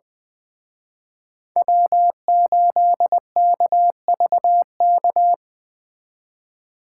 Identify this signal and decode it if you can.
Morse code, heard as a keyed tone, about 20 wpm, 700 Hz. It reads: W8KVK
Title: Speed Racing - USA 1x3 Call Signs 17wpm